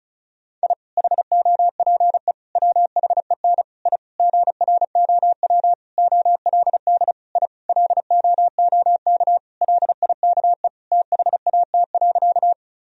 Morse code, heard as a keyed tone, about 34 wpm, 700 Hz. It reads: I HOPE WHEN I GROW OLD I LOOK LIKE THAT.
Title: Sentences from Top 1000 Words - Review Part 1 - 35wpm